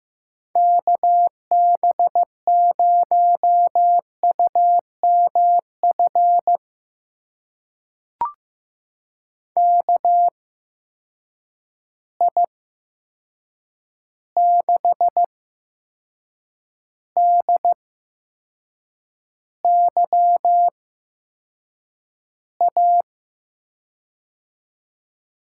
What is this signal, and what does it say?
Morse code, heard as a keyed tone, about 15 wpm, spaced as for 6 wpm, 700 Hz. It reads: KB0UMF KI6DYA
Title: USA Call Signs - 6 Characters - 8x Character Spacing - 15wpm